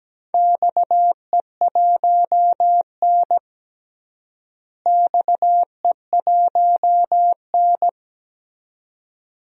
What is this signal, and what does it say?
Morse code, heard as a keyed tone, about 17 wpm, 700 Hz. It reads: XE1N XE1N
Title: DXCC-100 Call Signs - 4 Characters - Repeated 2x - 17wpm